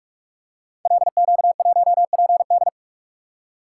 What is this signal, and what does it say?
Morse code, heard as a keyed tone, about 44 wpm, 700 Hz. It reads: LQ1PD
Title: Code Groups - 5 Letter:Numbers - 45wpm